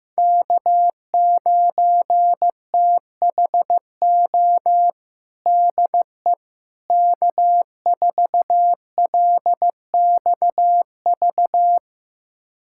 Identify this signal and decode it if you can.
Morse code, heard as a keyed tone, about 15 wpm, 700 Hz. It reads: K9THO DE K4LXV
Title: USA Callsign DE Callsign 15wpm